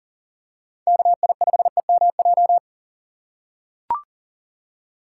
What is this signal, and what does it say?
Morse code, heard as a keyed tone, about 40 wpm, 700 Hz. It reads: KI5EMJ
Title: USA Call Signs - 6 Characters - 3x Character Spacing - 40wpm